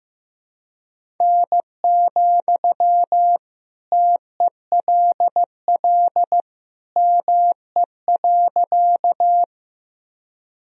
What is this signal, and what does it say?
Morse code, heard as a keyed tone, about 15 wpm, 700 Hz. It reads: N, TELL ME.
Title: Sentences from Top 200 Words and Abbreviations 15wpm